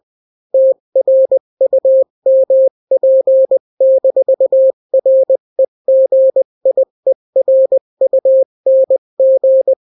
Morse code, heard as a keyed tone, about 20 wpm, 530 Hz. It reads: TRUMP-REGIERUNG
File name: daily / 18wpm